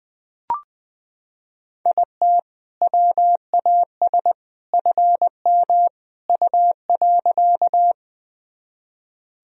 Morse code, heard as a keyed tone, about 21 wpm, 700 Hz. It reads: IT WAS FM U.
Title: Sentences from Top 100 Words and Abbreviations 20wpm